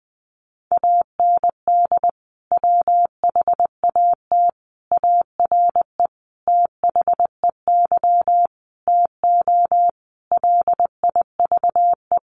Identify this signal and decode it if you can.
Morse code, heard as a keyed tone, about 20 wpm, 700 Hz. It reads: AND WHAT ARE THEY TO LIVE
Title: Sentences from Top 1100 Words 20wpm